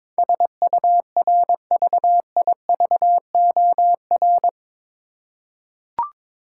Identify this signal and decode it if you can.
Morse code, heard as a keyed tone, about 22 wpm, 700 Hz. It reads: SURVIVOR
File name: practice / 22wpm / Eight Letter Words - 22wpm